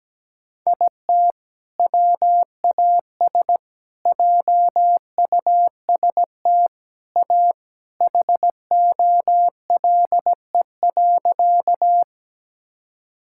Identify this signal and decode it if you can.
Morse code, heard as a keyed tone, about 17 wpm, 700 Hz. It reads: IT WAS JUST A HOLE.
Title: Sentences from Top 1600 Words 17wpm